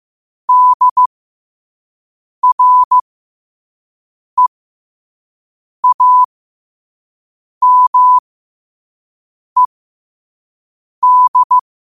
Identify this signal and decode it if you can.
Morse code, heard as a keyed tone, about 15 wpm, spaced as for 5 wpm, 1000 Hz. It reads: DREAMED
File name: qotdfiles / 5wpm